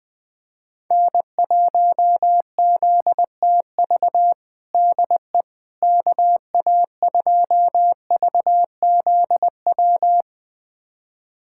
Morse code, heard as a keyed tone, about 20 wpm, 700 Hz. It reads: N1ZTV DE KA2VZW
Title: USA Callsign DE Callsign 20wpm